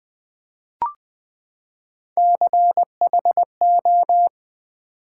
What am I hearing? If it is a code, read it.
Morse code, heard as a keyed tone, about 20 wpm, 700 Hz. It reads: CHO
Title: Build A Word - 3 Letters At A Time - 9-Letter Words - 20wpm